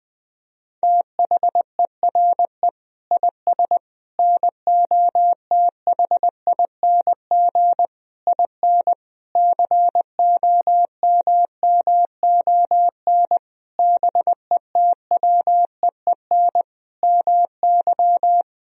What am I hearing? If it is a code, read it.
Morse code, heard as a keyed tone, about 20 wpm, 700 Hz. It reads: THERE IS NOTHING IN COMMON BETWEEN MY
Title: Sentences from Top 1000 Words - Review Part 3 - 20wpm